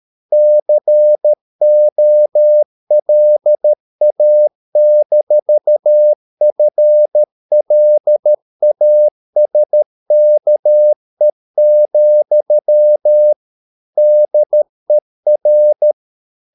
Morse code, heard as a keyed tone, about 13 wpm, 600 Hz. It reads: COLA-FLASKE, DER
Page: Ferie 13 wpm